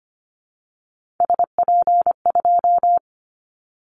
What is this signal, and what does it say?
Morse code, heard as a keyed tone, about 25 wpm, 700 Hz. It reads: SP2
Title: Callsign Prefixes for Top 25 DXCC - 25wpm